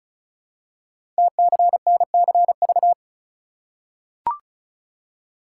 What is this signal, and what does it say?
Morse code, heard as a keyed tone, about 36 wpm, 690 Hz. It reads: TCNCV
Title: Code Groups - 5 Letters - 35wpm